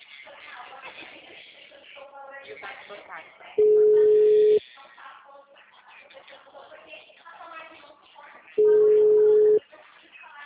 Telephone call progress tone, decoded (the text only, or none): ringback tone